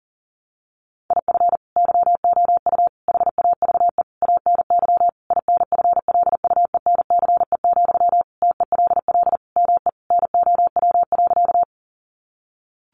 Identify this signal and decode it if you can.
Morse code, heard as a keyed tone, about 40 wpm, 700 Hz. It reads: IF YOU HAVE ANY INFLUENCE, TELL ME NOW.
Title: Sentences from Top 1100 Words 40wpm